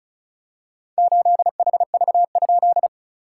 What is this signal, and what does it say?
Morse code, heard as a keyed tone, about 36 wpm, 690 Hz. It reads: 8HV?
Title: SOTA QSO Practice 35wpm